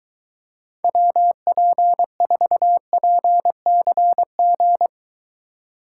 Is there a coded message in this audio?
Yes: Morse code from 0.8 to 4.9 s, heard as a keyed tone, about 23 wpm, 700 Hz. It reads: WP4PCG